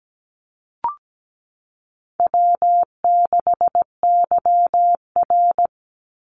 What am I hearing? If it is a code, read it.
Morse code, heard as a keyed tone, about 17 wpm, 700 Hz. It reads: W6YR
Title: Easy Call Signs Repeated 3x - 17wpm